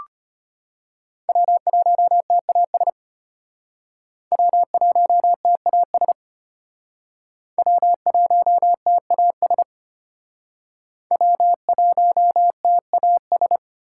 Morse code, heard as a keyed tone, about 33 wpm, 700 Hz. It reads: W1TAS W1TAS W1TAS W1TAS